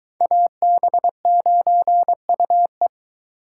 Morse code, heard as a keyed tone, about 23 wpm, 700 Hz. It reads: AB9UE